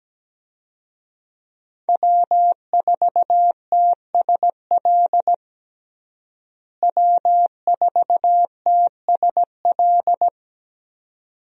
Morse code, heard as a keyed tone, about 17 wpm, 700 Hz. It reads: W4TSL W4TSL
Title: USA Call Signs - 5 Characters - Repeated 2x - 17wpm